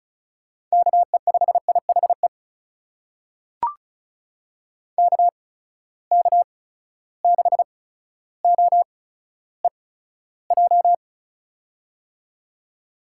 Morse code, heard as a keyed tone, about 36 wpm, spaced as for 15 wpm, 700 Hz. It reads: KE5IHE KK6OEJ
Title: USA Call Signs - 6 Characters - 8x Character Spacing - 35wpm